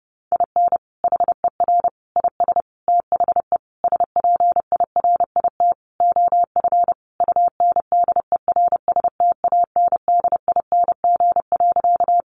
Morse code, heard as a keyed tone, about 30 wpm, 700 Hz. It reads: IN HER IS THE SPIRIT OF UNDERSTANDING.
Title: Sentences from Top 1400 Words 30wpm